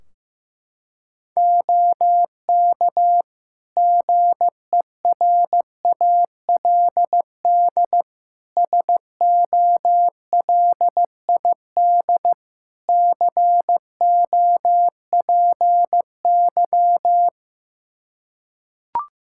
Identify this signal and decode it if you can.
Morse code, heard as a keyed tone, about 15 wpm, 700 Hz. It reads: OK GERALD SOLID COPY
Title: Practice QSO - Exchange 3 - More Info - 15wpm